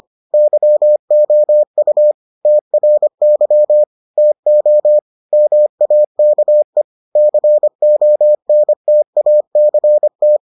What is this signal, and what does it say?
Morse code, heard as a keyed tone, about 25 wpm, 600 Hz. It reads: YOU TRY TO MAKE CONTACT